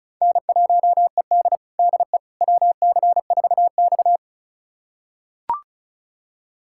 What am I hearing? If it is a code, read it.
Morse code, heard as a keyed tone, about 36 wpm, 700 Hz. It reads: N1ED DE WC4X
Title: USA Callsign DE Callsign - 4 Characters - 35wpm